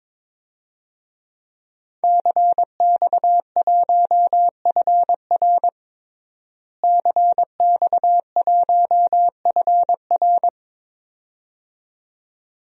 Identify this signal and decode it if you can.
Morse code, heard as a keyed tone, about 22 wpm, 700 Hz. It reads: CX1FR CX1FR
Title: DXCC-100 Call Signs - 5 Characters - Repeated 2x - 22wpm